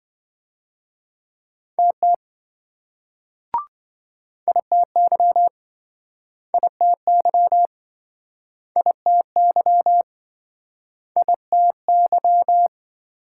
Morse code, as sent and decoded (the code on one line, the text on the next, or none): - - / .. - -.-- / .. - -.-- / .. - -.-- / .. - -.--
TT ITY ITY ITY ITY